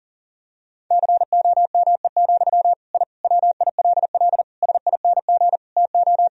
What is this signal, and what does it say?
Morse code, heard as a keyed tone, about 40 wpm, 700 Hz. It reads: COME, I WILL SING TO
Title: Sentences from Top 1000 Words - Review Part 2 - 40wpm